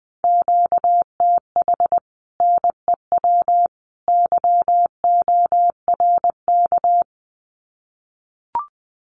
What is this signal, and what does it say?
Morse code, heard as a keyed tone, about 20 wpm, 700 Hz. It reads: QTH NEW YORK
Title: Common QSO Phrases 20wpm